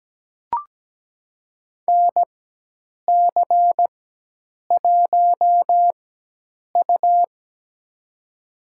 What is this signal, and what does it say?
Morse code, heard as a keyed tone, about 17 wpm, spaced as for 8 wpm, 700 Hz. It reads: NC1U
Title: USA Call Signs - 4 Characters - 4x Character Spacing - 17wpm